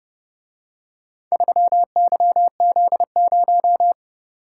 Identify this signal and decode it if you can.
Morse code, heard as a keyed tone, about 30 wpm, 700 Hz. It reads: 3YZ0